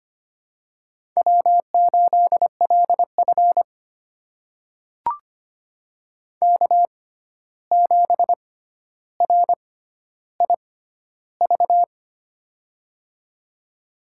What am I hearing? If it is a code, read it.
Morse code, heard as a keyed tone, about 25 wpm, spaced as for 12 wpm, 700 Hz. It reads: W8LF K7RIV
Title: USA Call Signs - 6x Character Spacing - 25wpm